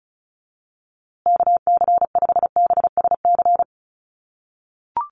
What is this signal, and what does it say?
Morse code, heard as a keyed tone, about 34 wpm, 700 Hz. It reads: KC5BSC
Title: USA Call Signs Repeated 3x 35wpm